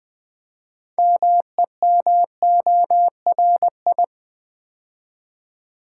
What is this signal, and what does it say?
Morse code, heard as a keyed tone, about 20 wpm, 700 Hz. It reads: MEMORI
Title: Build A Word - 2 Letters At A Time - 8-Letter Words - 20wpm